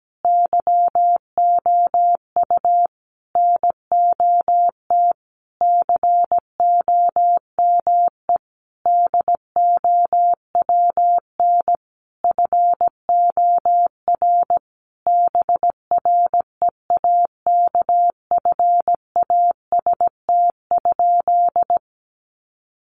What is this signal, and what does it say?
Morse code, heard as a keyed tone, about 17 wpm, 700 Hz. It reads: YOU NOT COME DOWN FOR BREAKFAST?
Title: Sentences from Top 1500 Words 17wpm